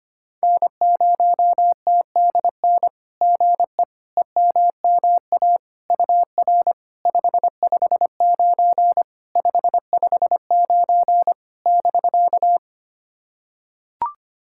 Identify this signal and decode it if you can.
Morse code, heard as a keyed tone, about 25 wpm, 700 Hz. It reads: N0TDN GE EMMA UR 559 559 <BK>
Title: SOTA QSO Practice 25wpm